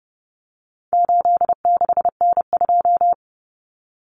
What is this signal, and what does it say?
Morse code, heard as a keyed tone, about 30 wpm, 700 Hz. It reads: 86N2
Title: Common Cut Numbers - 4-Digit 30wpm